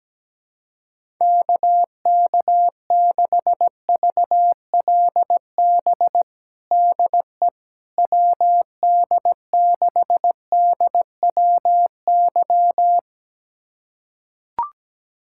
Morse code, heard as a keyed tone, about 17 wpm, 700 Hz. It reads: KK6VLB DE WD6DWY
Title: USA Callsign DE Callsign - 6 Characters - 17wpm